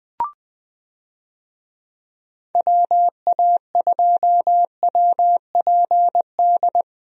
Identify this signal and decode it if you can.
Morse code, heard as a keyed tone, about 20 wpm, 700 Hz. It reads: WA2WPD
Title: USA Call Signs - 6 Characters - Repeated 2x - 20wpm